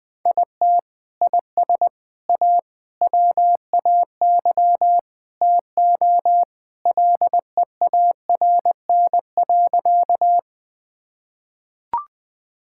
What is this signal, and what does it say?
Morse code, heard as a keyed tone, about 20 wpm, 700 Hz. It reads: IT IS A WAY TO LEARN.
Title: Sentences with 3-Word Phrases - 20wpm